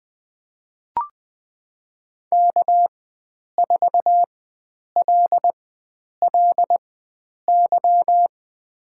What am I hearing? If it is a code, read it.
Morse code, heard as a keyed tone, about 20 wpm, spaced as for 9 wpm, 700 Hz. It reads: K4LLY
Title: USA Call Signs - 5 Characters - 4x Character Spacing - 20wpm